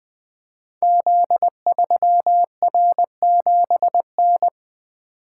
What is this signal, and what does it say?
Morse code, heard as a keyed tone, about 20 wpm, 700 Hz. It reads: Z3R7N
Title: Code Groups - 5 Letter:Numbers - 20wpm